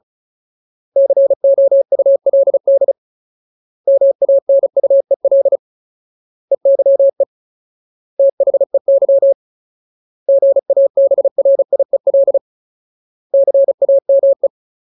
Morse code, heard as a keyed tone, about 35 wpm, 550 Hz. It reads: COULD MANUEL EYE THEY GABRIEL CAME
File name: Ultimate Word List / 35WPM Ultimate